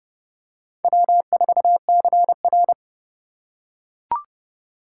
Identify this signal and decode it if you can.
Morse code, heard as a keyed tone, about 30 wpm, 700 Hz. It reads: W4CR
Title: USA Call Signs - 2x Character Spacing - 30wpm